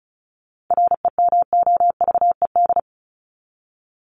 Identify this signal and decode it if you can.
Morse code, heard as a keyed tone, about 34 wpm, 700 Hz. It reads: REMOVED
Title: ED Suffix - Top 100 Words with Base Words 35wpm